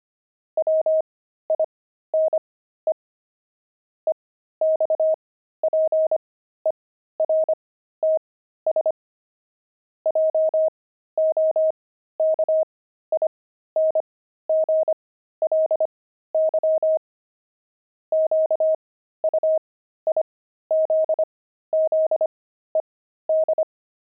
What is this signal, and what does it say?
Morse code, heard as a keyed tone, about 25 wpm, spaced as for 13 wpm, 630 Hz. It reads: WINE EXPERTS JOKINGLY QUIZZED
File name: pangram-fs-13-cs-25-wine